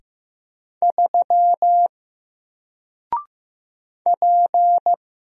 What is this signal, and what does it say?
Morse code, heard as a keyed tone, about 15 wpm, 700 Hz. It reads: 3 P
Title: Single Character (Letter:Number:Punctuation:Prosign) 15wpm